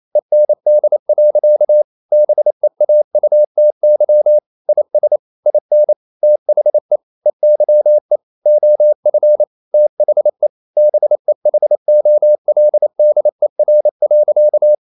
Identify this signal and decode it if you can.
Morse code, heard as a keyed tone, about 28 wpm, 600 Hz. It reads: END. BEAUTY IS IN THE EYE OF THE BEHOLDER.
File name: CW / Practice / EnglishSayings28